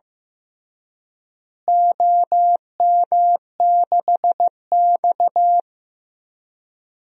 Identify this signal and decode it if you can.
Morse code, heard as a keyed tone, about 15 wpm, 700 Hz. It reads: OM6X
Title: DXCC-100 Call Signs - 4 Characters - Repeated 3x - 15wpm